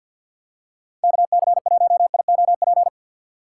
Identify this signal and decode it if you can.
Morse code, heard as a keyed tone, about 51 wpm, 700 Hz. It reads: KC1IOP
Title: USA Call Signs - 3x Character Spacing - 50wpm